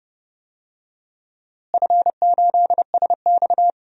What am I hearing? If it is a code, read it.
Morse code, heard as a keyed tone, about 30 wpm, 700 Hz. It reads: F8SX
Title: DXCC-100 Call Signs - 4 Characters - Repeated 2x - 30wpm